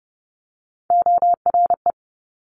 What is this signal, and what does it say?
Morse code, heard as a keyed tone, about 30 wpm, 700 Hz. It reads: ORE
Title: Top Three Letter Combinations in Words 30wpm